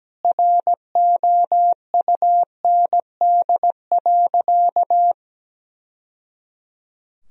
Morse code, heard as a keyed tone, about 17 wpm, 700 Hz. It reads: ROUND.
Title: Sentences from Top 1000 Words - Review Part 1 - 17wpm